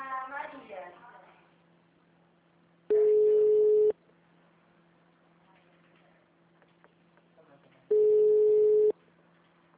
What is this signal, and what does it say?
Ringback tone (Europe, 425 Hz), with speech.
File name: out- -2011- - - .11489